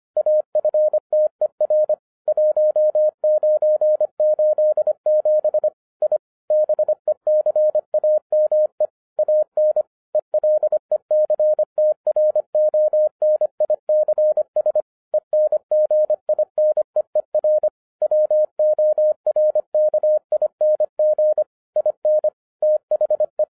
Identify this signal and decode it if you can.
Morse code, heard as a keyed tone, about 25 wpm, 610 Hz. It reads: AFTER 1987 I BECAME AN ELECTRONICS ENGINEER WORKING IN THE